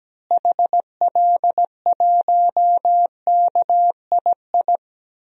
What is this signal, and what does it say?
Morse code, heard as a keyed tone, about 17 wpm, 700 Hz. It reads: HL1KII
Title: DXCC-100 Call Signs - 6 Characters - Repeated 2x - 17wpm